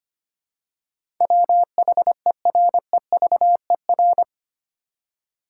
Morse code, heard as a keyed tone, about 25 wpm, 700 Hz. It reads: WHEREVER
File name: Eight Letter Words - 25wpm